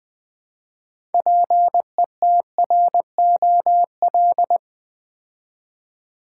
Morse code, heard as a keyed tone, about 20 wpm, 700 Hz. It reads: PETROL
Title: Build A Word - 3 Letters At A Time - 6-Letter Words - 20wpm